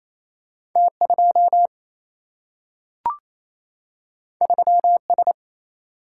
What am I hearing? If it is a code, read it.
Morse code, heard as a keyed tone, about 28 wpm, 700 Hz. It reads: T 2 3 S
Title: Single-Letter Single-Number 28wpm